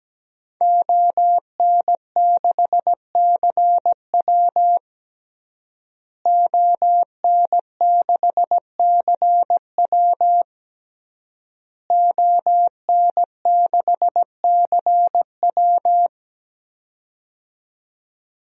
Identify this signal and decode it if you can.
Morse code, heard as a keyed tone, about 17 wpm, 700 Hz. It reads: ON6CW ON6CW ON6CW
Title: DXCC-100 Call Signs - 5 Characters - Repeated 3x - 17wpm